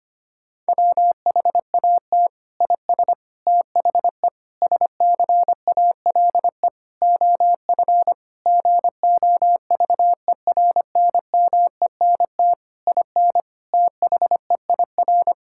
Morse code, heard as a keyed tone, about 25 wpm, 700 Hz. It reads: WHAT IS THE SCALE OF GOVERNMENT IN THEIR
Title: Sentences from Top 1000 Words - Review Part 5 - 25wpm